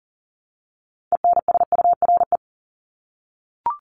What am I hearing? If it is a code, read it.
Morse code, heard as a keyed tone, about 40 wpm, 700 Hz. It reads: ENSURE
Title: Top 1001-1250 Words 40wpm